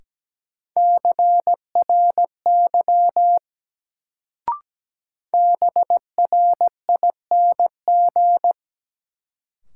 Morse code, heard as a keyed tone, about 17 wpm, 700 Hz. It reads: CRY BRING
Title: Sets of 5 Words Encore - New Word Drill - 17wpm